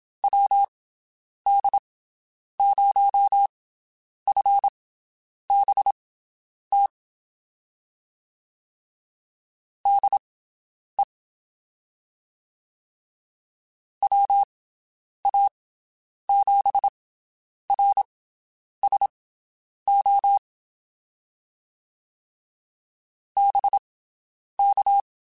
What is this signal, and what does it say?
Morse code, heard as a keyed tone, about 27 wpm, spaced as for 9 wpm, 800 Hz. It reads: WD0FBT DE WA7RSO BK
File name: QSOs Spaced